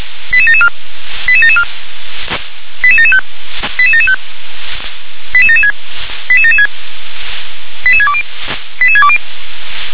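Five-tone selective call (ZVEI), heard as quick runs of five tones, tones of about 70 ms. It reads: 88774 88774 88775 88775 88776 88776 88620 88620
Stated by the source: ZVEI five-tone call 88771 received by SDR.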